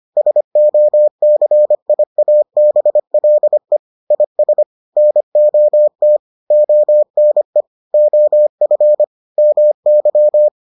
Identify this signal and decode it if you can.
Morse code, heard as a keyed tone, about 25 wpm, 600 Hz. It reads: SOCIABLE IS NOT ONE OF MY